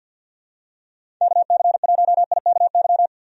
Morse code, heard as a keyed tone, about 52 wpm, 700 Hz. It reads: KC1IKY
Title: USA Call Signs - 6 Characters - 50wpm